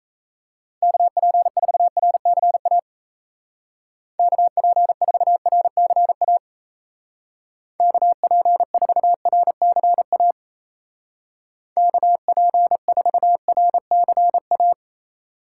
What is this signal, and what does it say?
Morse code, heard as a keyed tone, about 34 wpm, 700 Hz. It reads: KP4RCA KP4RCA KP4RCA KP4RCA
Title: Speed Racing - USA 2x3 Call Signs 28wpm